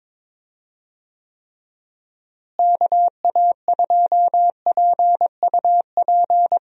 Morse code, heard as a keyed tone, about 22 wpm, 700 Hz. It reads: KA2PUP
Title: USA Call Signs - 6 Characters - Repeated 3x - 22wpm